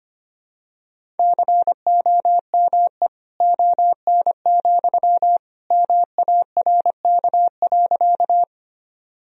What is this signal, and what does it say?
Morse code, heard as a keyed tone, about 25 wpm, 700 Hz. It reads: COME ON, MARK.
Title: Sentences from Top 1000 Words - Review Part 1 - 25wpm